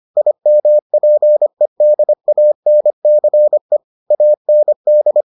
Morse code, heard as a keyed tone, about 25 wpm, 600 Hz. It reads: IMPEDANCE AND